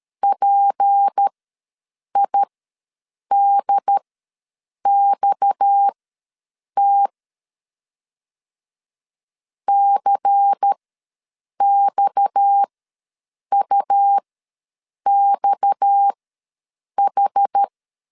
Morse code, heard as a keyed tone, about 13 wpm, spaced as for 7 wpm, 800 Hz. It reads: PIDXT CXUXH
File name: LEZIONI / 6 PKX0 / 6 PKX ws 12wpm 3106